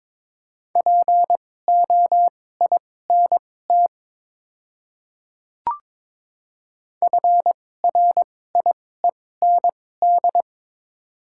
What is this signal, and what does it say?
Morse code, heard as a keyed tone, about 22 wpm, spaced as for 16 wpm, 700 Hz. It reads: POINT FRIEND
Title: 2x Character Spacing - Top 300 Words - 22wpm